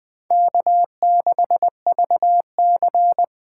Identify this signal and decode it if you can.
Morse code, heard as a keyed tone, about 20 wpm, 700 Hz. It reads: K6VC
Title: Speed Racing - USA 1x2 Call Signs 17wpm